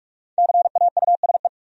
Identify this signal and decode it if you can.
Morse code, heard as a keyed tone, about 59 wpm, spaced as for 52 wpm, 700 Hz. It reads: CAUSE
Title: Speed-Racing Top 200 Words 45wpm